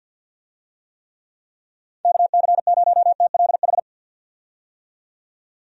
Morse code, heard as a keyed tone, about 51 wpm, 700 Hz. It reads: KC0TLH
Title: USA Call Signs - 6 Characters - 50wpm